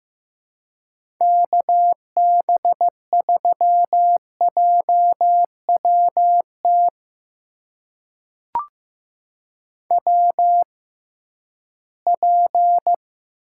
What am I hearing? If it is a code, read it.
Morse code, heard as a keyed tone, about 15 wpm, spaced as for 11 wpm, 700 Hz. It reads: KB3JWT WP
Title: USA Call Signs - 6 Characters - 6x Character Spacing - 15wpm